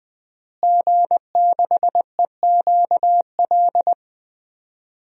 Morse code, heard as a keyed tone, about 20 wpm, 700 Hz. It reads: G6EQL
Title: DXCC-100 Call Signs - 5 Characters - Repeated 3x - 20wpm